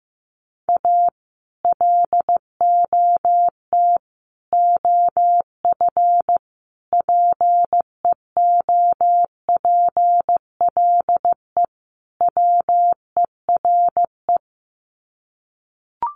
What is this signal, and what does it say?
Morse code, heard as a keyed tone, about 15 wpm, 700 Hz. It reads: A LOT OF PEOPLE WERE
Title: Sets of 5 Words Encore2 15wpm